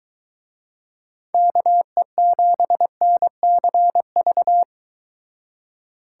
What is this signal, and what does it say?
Morse code, heard as a keyed tone, about 23 wpm, 700 Hz. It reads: KE7NCV